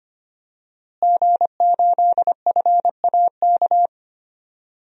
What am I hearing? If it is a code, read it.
Morse code, heard as a keyed tone, about 25 wpm, 700 Hz. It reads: G8FAK
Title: DXCC-100 Call Signs - 5 Characters - Repeated 2x - 25wpm